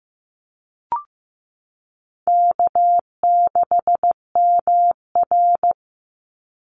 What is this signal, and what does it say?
Morse code, heard as a keyed tone, about 15 wpm, 700 Hz. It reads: K6MR
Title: Easy Call Signs - 15wpm